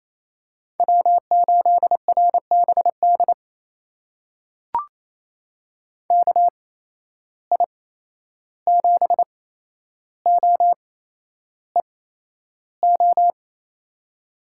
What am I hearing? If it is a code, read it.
Morse code, heard as a keyed tone, about 28 wpm, spaced as for 11 wpm, 700 Hz. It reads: W8RBD KI7OEO
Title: USA Call Signs - 8x Character Spacing - 28wpm